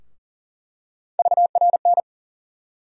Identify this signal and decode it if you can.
Morse code, heard as a keyed tone, about 40 wpm, 700 Hz. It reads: VRN
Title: USA Call Sign Postfixes - 3 Letters 40wpm